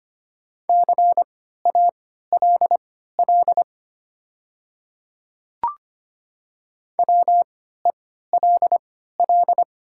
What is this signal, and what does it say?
Morse code, heard as a keyed tone, about 25 wpm, spaced as for 14 wpm, 700 Hz. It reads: CALL WELL
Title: 3x Character Spacing - Top 200 Words - 25wpm